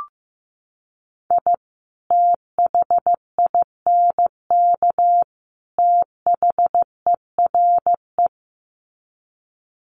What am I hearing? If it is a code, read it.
Morse code, heard as a keyed tone, about 15 wpm, 700 Hz. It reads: I THINK THERE
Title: Sets of 3 Words 15wpm